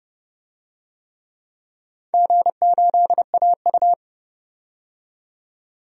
Morse code, heard as a keyed tone, about 30 wpm, 700 Hz. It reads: G8AU
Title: DXCC-100 Call Signs - 4 Characters - 30wpm